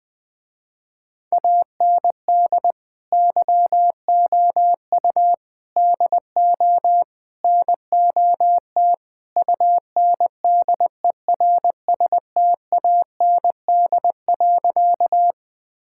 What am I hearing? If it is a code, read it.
Morse code, heard as a keyed tone, about 21 wpm, 700 Hz. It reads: AND YOU DO NOT UNDERSTAND.
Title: Sentences with 4-Word Phrases - Spoken Beginning - 20wpm